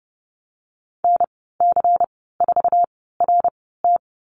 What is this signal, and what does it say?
Morse code, heard as a keyed tone, about 30 wpm, spaced as for 17 wpm, 700 Hz. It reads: NC4RT
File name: Lesson 19g - Letter w - Call Signs 30wpm